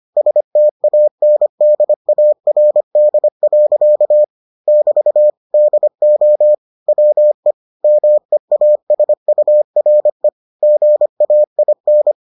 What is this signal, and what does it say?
Morse code, heard as a keyed tone, about 25 wpm, 600 Hz. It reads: STANDARD. <BT> DO WE MEASURE GAIN